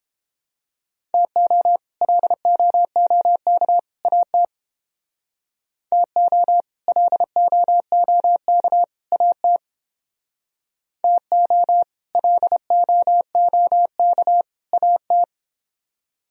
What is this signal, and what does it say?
Morse code, heard as a keyed tone, about 29 wpm, 700 Hz. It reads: TO LOOK AT TO LOOK AT TO LOOK AT